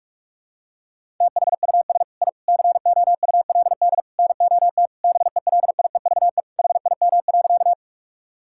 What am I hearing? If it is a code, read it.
Morse code, heard as a keyed tone, about 46 wpm, 700 Hz. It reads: THUS I COULD NOT BELIEVE HIM.
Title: Sentences from Top 1000 Words - Review Part 5 - 45wpm